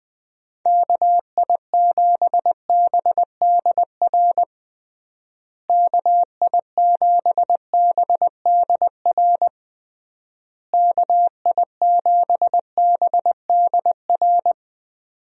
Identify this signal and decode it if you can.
Morse code, heard as a keyed tone, about 19 wpm, 700 Hz. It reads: KI7BDR KI7BDR KI7BDR
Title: USA Call Signs - 6 Characters - Repeated 3x - 20wpm